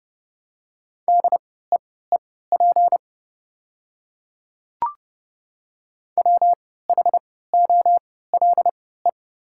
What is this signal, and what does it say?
Morse code, heard as a keyed tone, about 30 wpm, spaced as for 17 wpm, 700 Hz. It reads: DEEP WHOLE
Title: 3x Character Spacing - Top 500 Words - 30wpm